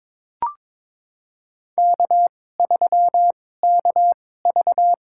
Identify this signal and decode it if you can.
Morse code, heard as a keyed tone, about 22 wpm, spaced as for 16 wpm, 700 Hz. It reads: K3KV
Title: USA Call Signs - 2x Character Spacing - 22wpm